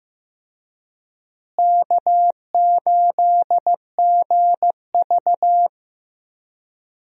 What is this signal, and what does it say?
Morse code, heard as a keyed tone, about 15 wpm, 700 Hz. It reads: K8GV
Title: USA Call Signs - 4 Characters - Repeated 2x - 15wpm